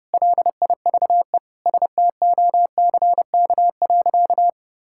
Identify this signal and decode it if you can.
Morse code, heard as a keyed tone, about 30 wpm, 700 Hz. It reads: LIVE STOCK.
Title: Sentences from Top 1000 Words - Review Part 5 - 30wpm